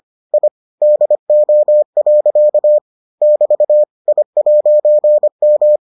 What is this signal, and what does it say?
Morse code, heard as a keyed tone, about 25 wpm, 600 Hz. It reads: I DO. <BT> I'M